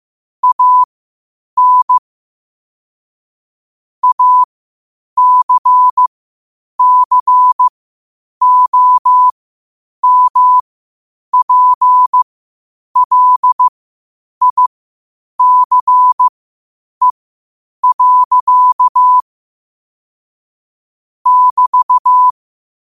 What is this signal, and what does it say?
Morse code, heard as a keyed tone, about 15 wpm, spaced as for 8 wpm, 1000 Hz. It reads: AN ACCOMPLICE. <BT>
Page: New quotes every day in morse code at 8 Words per minute.